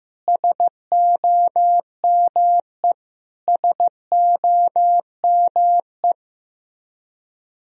Morse code, heard as a keyed tone, about 16 wpm, 700 Hz. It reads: SOME SOME
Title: Top 100 Words Repeated 15wpm